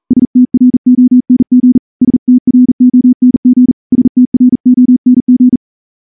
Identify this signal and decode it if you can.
Morse code, heard as a keyed tone, about 38 wpm, 250 Hz. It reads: STRONG STRONG STRONG
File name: morse-training-40WPM